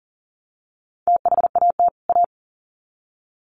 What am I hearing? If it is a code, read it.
Morse code, heard as a keyed tone, about 40 wpm, 700 Hz. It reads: THAT A
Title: Sets of 2 Words 40wpm